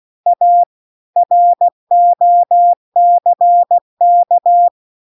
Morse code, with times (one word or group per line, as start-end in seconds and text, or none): 0.3-0.6 A
1.2-4.7 ROCK